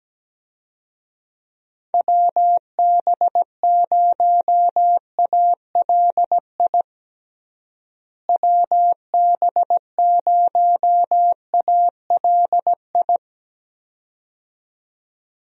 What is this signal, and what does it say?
Morse code, heard as a keyed tone, about 17 wpm, 700 Hz. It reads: WB0ALI WB0ALI
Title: USA Call Signs - 6 Characters - Repeated 2x - 17wpm